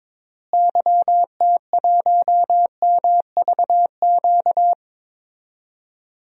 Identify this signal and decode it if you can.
Morse code, heard as a keyed tone, about 22 wpm, 700 Hz. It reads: YT1MVQ
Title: DXCC-100 Call Signs - 6 Characters - Repeated 3x - 22wpm